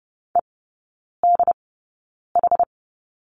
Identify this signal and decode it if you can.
Morse code, heard as a keyed tone, about 30 wpm, spaced as for 9 wpm, 700 Hz. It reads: EDH
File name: Lesson 15e - Number 5 - Three Characters 30wpm